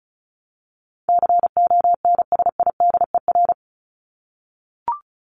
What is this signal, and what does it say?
Morse code, heard as a keyed tone, about 34 wpm, 700 Hz. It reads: CONSIDER
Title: ED Suffix - Top 100 Words with Base Words 35wpm